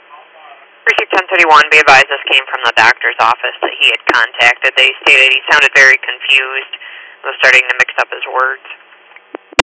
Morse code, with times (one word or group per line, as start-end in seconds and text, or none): none